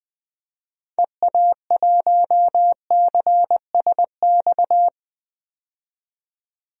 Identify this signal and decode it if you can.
Morse code, heard as a keyed tone, about 21 wpm, 700 Hz. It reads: EA1CSX
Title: DXCC-100 Call Signs - 6 Characters - 20wpm